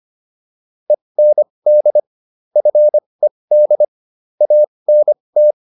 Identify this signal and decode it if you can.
Morse code, heard as a keyed tone, about 25 wpm, spaced as for 20 wpm, 600 Hz. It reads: END FED ANT